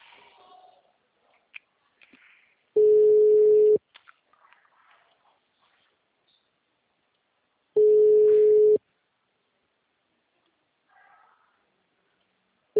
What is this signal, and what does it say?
Ringback tone (Europe, 425 Hz).
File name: out- -2012- - - .2